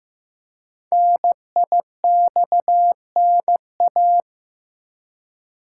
Morse code, heard as a keyed tone, about 15 wpm, 700 Hz. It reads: NIXNA
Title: Code Groups - 5 Letters - 15wpm